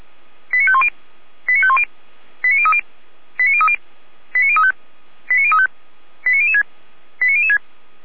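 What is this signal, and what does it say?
Five-tone selective call (ZVEI), heard as quick runs of five tones, tones of about 70 ms. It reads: 89510 89510 89030 89030 89035 89035 89006 89006
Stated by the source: Five-tone ZVEI selective call received by SDR.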